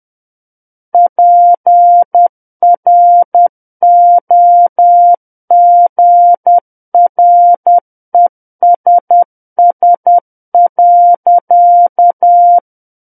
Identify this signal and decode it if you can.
Morse code, heard as a keyed tone, about 10 wpm, 700 Hz. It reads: PROGRESS.
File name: Book One / 01-Chapter 1 - 10 WPM